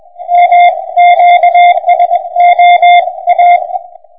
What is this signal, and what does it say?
Morse code, heard as a keyed tone, about 20 wpm, 720 Hz, with radio noise. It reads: MQSOA
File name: pa0oos / VK3IO 160